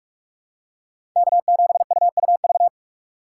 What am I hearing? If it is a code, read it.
Morse code, heard as a keyed tone, about 46 wpm, 700 Hz. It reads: K7UUV
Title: USA Call Signs - 5 Characters - 2x Character Spacing - 45wpm